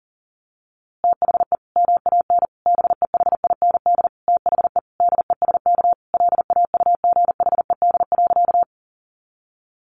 Morse code, heard as a keyed tone, about 40 wpm, 700 Hz. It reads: THE MAN BEHIND THE DESK LAUGHED.
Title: A Sound Of Thunder 40wpm